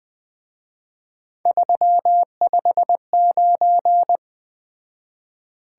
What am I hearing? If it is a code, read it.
Morse code, heard as a keyed tone, about 20 wpm, 700 Hz. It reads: 359
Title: Speed-Racing Top 180 QSO Elements 17wpm